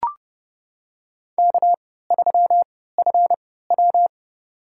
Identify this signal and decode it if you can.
Morse code, heard as a keyed tone, about 30 wpm, spaced as for 17 wpm, 700 Hz. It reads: K3FW
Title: USA Call Signs - 4 Characters - 3x Character Spacing - 30wpm